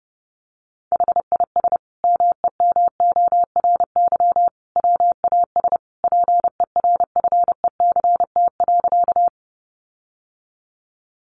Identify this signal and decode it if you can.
Morse code, heard as a keyed tone, about 30 wpm, 700 Hz. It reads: HIS MEMORY WAS PERFECT.
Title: Sentences from Top 1300 Words 30wpm